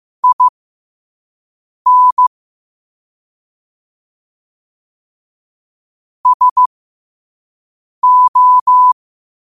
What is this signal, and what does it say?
Morse code, heard as a keyed tone, about 15 wpm, spaced as for 5 wpm, 1000 Hz. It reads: IN SO
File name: qotdfiles / 5wpm